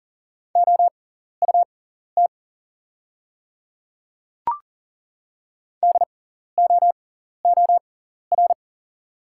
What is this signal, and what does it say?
Morse code, heard as a keyed tone, about 40 wpm, spaced as for 14 wpm, 690 Hz. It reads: OUT DOOR
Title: 6x Character Spacing - Top 500 Words - 40wpm